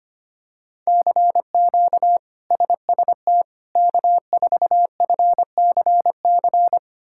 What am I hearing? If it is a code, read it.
Morse code, heard as a keyed tone, about 25 wpm, 700 Hz. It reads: CQ SST K4FCC
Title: K1USN SST QSO Practice - 25wpm